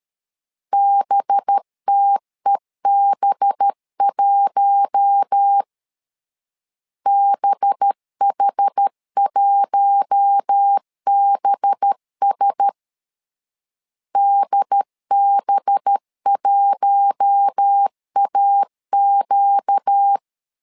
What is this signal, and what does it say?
Morse code, heard as a keyed tone, about 13 wpm, 800 Hz. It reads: BTEB1 BH1BS DB1AQ
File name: LEZIONI / 5 LBVC / 5 LB 12wpm 1906